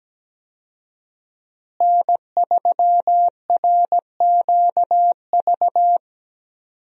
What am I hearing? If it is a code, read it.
Morse code, heard as a keyed tone, about 17 wpm, 700 Hz. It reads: N3RQV
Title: USA Call Signs - 5 Characters - Repeated 3x - 17wpm